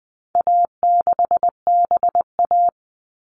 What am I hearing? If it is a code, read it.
Morse code, heard as a keyed tone, about 20 wpm, 700 Hz. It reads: A6BA
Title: All Cut Numbers - 4-Digit 20wpm